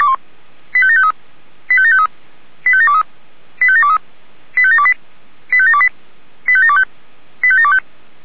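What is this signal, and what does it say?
Five-tone selective call (ZVEI), heard as quick runs of five tones, tones of about 70 ms. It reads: 85753 85753 85723 85723 85728 85728 85725 85725
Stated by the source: ZVEI five-tone selective call received on SDR.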